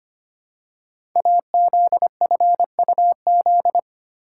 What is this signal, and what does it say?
Morse code, heard as a keyed tone, about 25 wpm, 700 Hz. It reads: AZFUZ
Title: Code Groups - 5 Letters - 25wpm